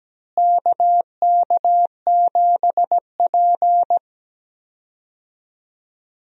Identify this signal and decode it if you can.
Morse code, heard as a keyed tone, about 17 wpm, 700 Hz. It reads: KK7P
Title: USA Call Signs - 4 Characters - Repeated 2x - 17wpm